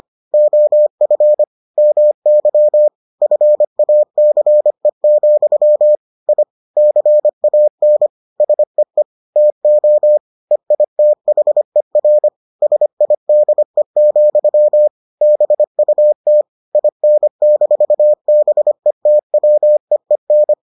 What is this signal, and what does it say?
Morse code, heard as a keyed tone, about 25 wpm, 600 Hz. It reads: OF MY FACE, I CAN SEE TO EITHER SIDE, BUT IN-BETWEEN